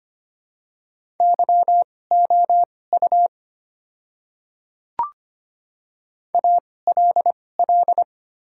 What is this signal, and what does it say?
Morse code, heard as a keyed tone, about 25 wpm, spaced as for 18 wpm, 700 Hz. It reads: YOU ALL
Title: 2x Character Spacing - Top 300 Words - 25wpm